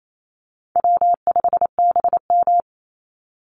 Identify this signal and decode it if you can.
Morse code, heard as a keyed tone, about 28 wpm, 700 Hz. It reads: W5BM
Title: Easy Call Signs Repeated 3x - 28wpm